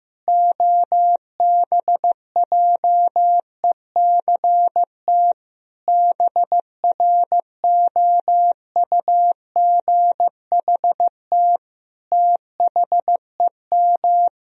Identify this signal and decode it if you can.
Morse code, heard as a keyed tone, about 15 wpm, 700 Hz. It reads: OBJECT BROUGHT THEM
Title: Sentences from Top 1000 Words - Review Part 3 - 15wpm